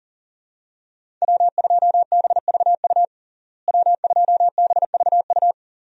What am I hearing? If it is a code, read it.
Morse code, heard as a keyed tone, about 40 wpm, 700 Hz. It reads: W2BVU W2BVU
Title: USA Call Signs - 5 Characters - Repeated 2x - 40wpm